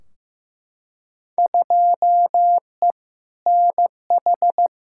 Morse code, heard as a keyed tone, about 15 wpm, 700 Hz. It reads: 2E NH
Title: Field Day Exchanges - 15wpm